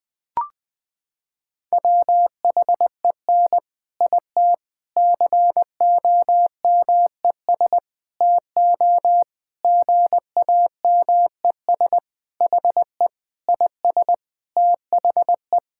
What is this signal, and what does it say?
Morse code, heard as a keyed tone, about 20 wpm, 700 Hz. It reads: WHEN IT COMES TO GAMES HE IS THE
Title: Sentences with 3-Word Phrases - 20wpm